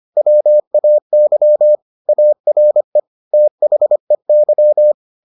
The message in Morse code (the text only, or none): WAY ARE THEY